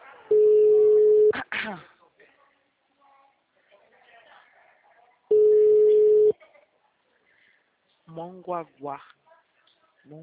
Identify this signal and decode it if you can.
Ringback tone (Europe, 425 Hz), with speech.